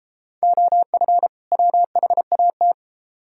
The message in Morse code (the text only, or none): OF WHAT